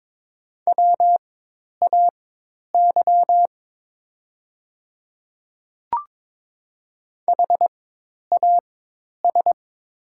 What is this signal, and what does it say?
Morse code, heard as a keyed tone, about 22 wpm, spaced as for 10 wpm, 700 Hz. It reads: WAY HAS
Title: 4x Character Spacing - Top 100 Words - 22wpm